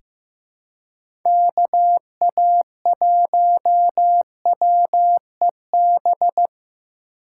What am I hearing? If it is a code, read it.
Morse code, heard as a keyed tone, about 15 wpm, 700 Hz. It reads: KA1WEB
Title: USA Call Signs - 6 Characters - 2x Character Spacing - 15wpm